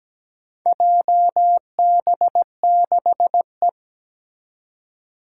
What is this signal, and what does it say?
Morse code, heard as a keyed tone, about 17 wpm, 700 Hz. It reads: JB6E
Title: Four Challenging Letter-Number Combinations 17wpm